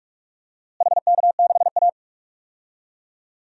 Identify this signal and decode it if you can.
Morse code, heard as a keyed tone, about 47 wpm, 700 Hz. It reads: HK6A
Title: DXCC-100 Call Signs - 4 Characters - Repeated 2x - 45wpm